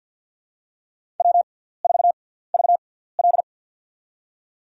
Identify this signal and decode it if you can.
Morse code, heard as a keyed tone, about 53 wpm, spaced as for 17 wpm, 700 Hz. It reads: W4VL
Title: USA Call Signs - 4 Characters - 6x Character Spacing - 50wpm